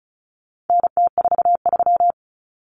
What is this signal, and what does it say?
Morse code, heard as a keyed tone, about 34 wpm, 700 Hz. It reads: NT43
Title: Common Cut Numbers - 4-Digit 35wpm